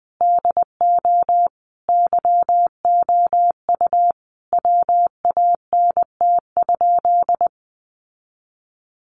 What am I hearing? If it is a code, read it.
Morse code, heard as a keyed tone, about 20 wpm, 700 Hz. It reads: DO YOU WANT?
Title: Sentences from Top 800 Words 20wpm